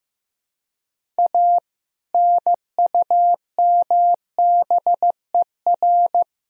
Morse code, heard as a keyed tone, about 15 wpm, 700 Hz. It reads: A NUMBER
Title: Sentences from Top 1000 Words - Review Part 1 - 15wpm